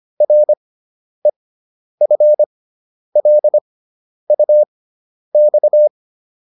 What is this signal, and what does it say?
Morse code, heard as a keyed tone, about 25 wpm, spaced as for 10 wpm, 600 Hz. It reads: REFLUX